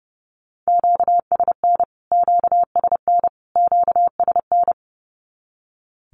Morse code, heard as a keyed tone, about 30 wpm, 700 Hz. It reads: QSN QSN QSN
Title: Q-Codes Repeated 3x 30wpm